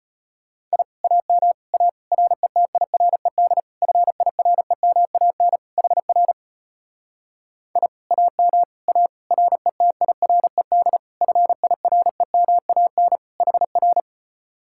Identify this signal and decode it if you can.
Morse code, heard as a keyed tone, about 33 wpm, 700 Hz. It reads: I AM A RETIRED FIREMAN HR I AM A RETIRED FIREMAN HR